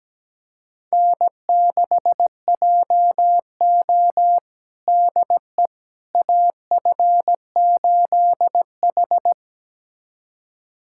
Morse code, heard as a keyed tone, about 17 wpm, 700 Hz. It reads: N6JO DE AF8H
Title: USA Callsign DE Callsign - 4 Characters - 17wpm